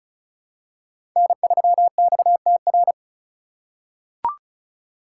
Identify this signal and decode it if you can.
Morse code, heard as a keyed tone, about 34 wpm, 700 Hz. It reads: N3XTR
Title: USA Call Signs - 2x Character Spacing - 35wpm